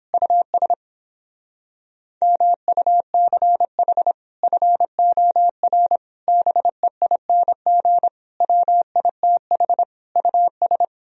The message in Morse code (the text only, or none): US MUCH FOR BEING WITH US